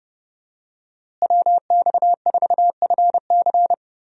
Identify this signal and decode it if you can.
Morse code, heard as a keyed tone, about 30 wpm, 700 Hz. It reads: WX4FC
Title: USA Call Signs - 5 Characters - Repeated 3x - 30wpm